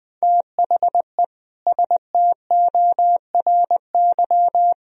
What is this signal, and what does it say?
Morse code, heard as a keyed tone, about 20 wpm, 700 Hz. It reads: THE STORY
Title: Sentences from Top 1000 Words - Review Part 1 - 20wpm